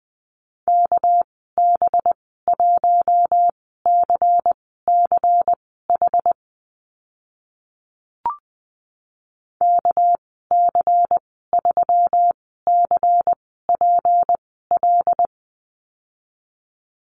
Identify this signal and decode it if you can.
Morse code, heard as a keyed tone, about 20 wpm, spaced as for 14 wpm, 700 Hz. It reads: KB1CCH KC3CPL
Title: 207 Lesson 33g - Letter j - Call Signs 20wpm